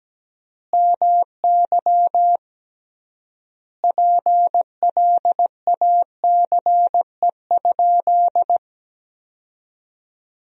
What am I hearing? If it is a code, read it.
Morse code, heard as a keyed tone, about 17 wpm, 700 Hz. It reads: MY PLACE?
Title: 3x Word-Spacing - Sentences from Top 200 Words - 17wpm